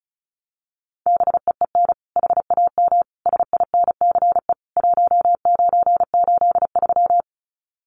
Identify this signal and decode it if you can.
Morse code, heard as a keyed tone, about 34 wpm, 700 Hz. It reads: BEEN HAM SINCE 1983
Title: Common QSO Phrases 35wpm